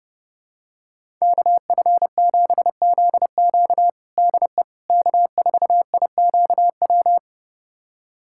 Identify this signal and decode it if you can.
Morse code, heard as a keyed tone, about 32 wpm, 700 Hz. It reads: KF7ZQ DE K4IQW
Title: USA Callsign DE Callsign - 5 Characters - 30wpm